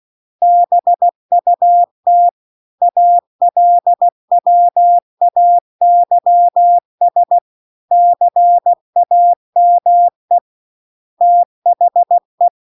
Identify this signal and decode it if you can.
Morse code, heard as a keyed tone, about 16 wpm, 700 Hz. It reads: BUT ALWAYS CAME THE
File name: CW-The-Big-Twoheaded-River-2-16WPM